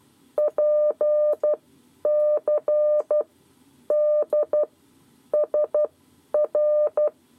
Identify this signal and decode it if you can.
Morse code, heard as a keyed tone, about 12 wpm, spaced as for 9 wpm, 600 Hz. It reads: PCDSR
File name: 7 Lezione cw 2